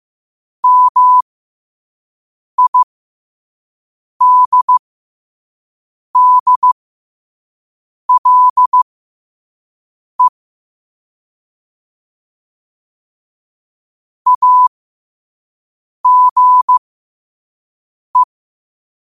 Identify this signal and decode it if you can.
Morse code, heard as a keyed tone, about 15 wpm, spaced as for 5 wpm, 1000 Hz. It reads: MIDDLE AGE
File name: qotdfiles / 5wpm